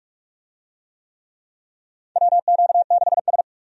Morse code, heard as a keyed tone, about 46 wpm, 700 Hz. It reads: WQ6S
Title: USA Call Signs - 4 Characters - 45wpm